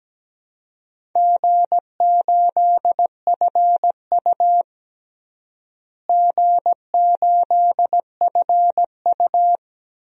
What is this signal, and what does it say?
Morse code, heard as a keyed tone, about 17 wpm, 700 Hz. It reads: G8FU G8FU
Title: DXCC-100 Call Signs - 4 Characters - Repeated 3x - 17wpm